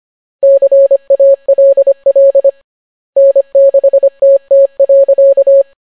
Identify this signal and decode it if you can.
Morse code, heard as a keyed tone, about 25 wpm, 550 Hz. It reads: CALL N6TT.